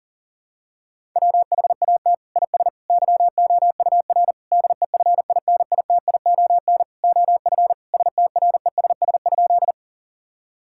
Morse code, heard as a keyed tone, about 40 wpm, 700 Hz. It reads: WHAT IS YOUR DEFINITION OF STRESS?
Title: Sentences from Top 1000 Words - Review Part 6 - 40wpm